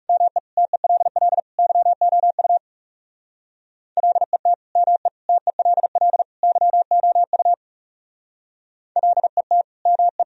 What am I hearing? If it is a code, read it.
Morse code, heard as a keyed tone, about 35 wpm, 700 Hz. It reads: ME TELL YOU LET ME TELL YOU LET ME
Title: Speed-Racing - Sets of 4 Words 30wpm